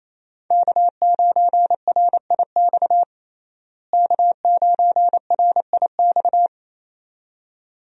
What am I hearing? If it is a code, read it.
Morse code, heard as a keyed tone, about 28 wpm, 700 Hz. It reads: K9RIX K9RIX
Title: USA Call Signs - 5 Characters - Repeated 2x - 28wpm